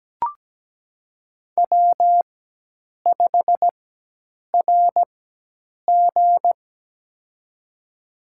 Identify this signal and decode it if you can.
Morse code, heard as a keyed tone, about 17 wpm, spaced as for 8 wpm, 700 Hz. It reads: W5RG
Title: USA Call Signs - 4x Character Spacing - 17wpm